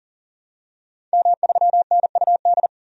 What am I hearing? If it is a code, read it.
Morse code, heard as a keyed tone, about 40 wpm, 700 Hz. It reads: M3NUD
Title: DXCC-100 Call Signs - 5 Characters - 40wpm